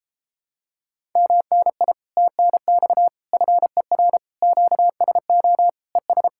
Morse code, heard as a keyed tone, about 33 wpm, 700 Hz. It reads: MNI TNX FER QSO ES